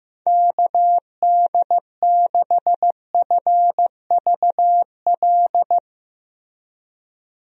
Morse code, heard as a keyed tone, about 15 wpm, 700 Hz. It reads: KD6FVL
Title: USA Call Signs - 6 Characters - Repeated 2x - 15wpm